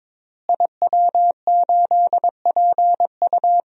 Morse code, heard as a keyed tone, about 22 wpm, 700 Hz. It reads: IW8PU
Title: DXCC-100 Call Signs - 5 Characters - Repeated 2x - 22wpm